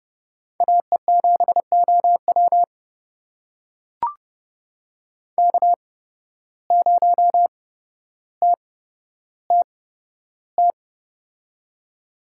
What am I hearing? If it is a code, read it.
Morse code, heard as a keyed tone, about 30 wpm, spaced as for 13 wpm, 700 Hz. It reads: AE7OW K0TTT
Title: USA Call Signs - 5 Characters - 8x Character Spacing - 30wpm